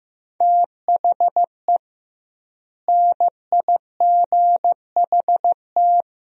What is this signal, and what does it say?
Morse code, heard as a keyed tone, about 16 wpm, 700 Hz. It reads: THE NIGHT
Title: 2x Word-Spacing - Sentences from Top 400 Words - 15wpm